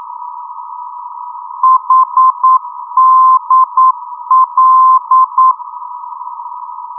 Morse code, heard as a keyed tone, about 9 wpm, 1040 Hz. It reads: HDL
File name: europe / d-8060-cw-id-2004-rb